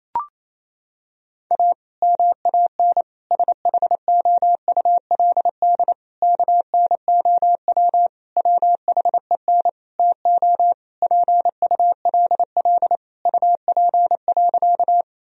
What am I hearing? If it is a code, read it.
Morse code, heard as a keyed tone, about 28 wpm, 700 Hz. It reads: A MAN SHOULD KNOW WHEN TO PULL UP.
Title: Sentences from Top 1000 Words - Review Part 2 - 28wpm